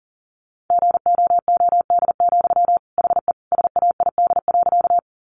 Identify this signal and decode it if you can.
Morse code, heard as a keyed tone, about 37 wpm, 700 Hz. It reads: GOOD, HE SAID.
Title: Sentences from Top 200 Words 40wpm